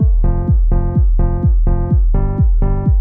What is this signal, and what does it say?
There is no code.